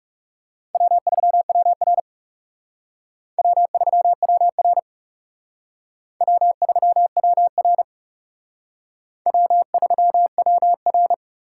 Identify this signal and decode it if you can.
Morse code, heard as a keyed tone, about 37 wpm, 700 Hz. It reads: W3WR W3WR W3WR W3WR
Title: Speed Racing - USA 1x2 Call Signs 30wpm